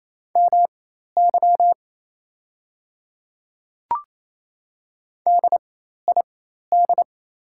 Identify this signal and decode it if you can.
Morse code, heard as a keyed tone, about 27 wpm, spaced as for 13 wpm, 690 Hz. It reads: MY DID
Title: 4x Character Spacing - Top 100 Words - 28wpm